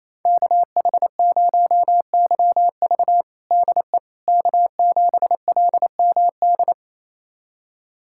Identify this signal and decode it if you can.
Morse code, heard as a keyed tone, about 28 wpm, 700 Hz. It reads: KH0YV DE K7LMD
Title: USA Callsign DE Callsign - 5 Characters - 28wpm